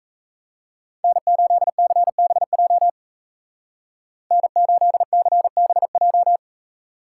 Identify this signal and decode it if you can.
Morse code, heard as a keyed tone, about 34 wpm, 700 Hz. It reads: N8CBJ N8CBJ
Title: Speed Racing - USA 1x3 Call Signs 28wpm